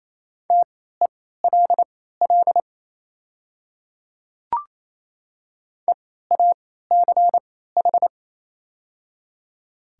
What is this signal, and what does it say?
Morse code, heard as a keyed tone, about 28 wpm, spaced as for 16 wpm, 700 Hz. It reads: TELL EACH
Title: 3x Character Spacing - Top 200 Words - 28wpm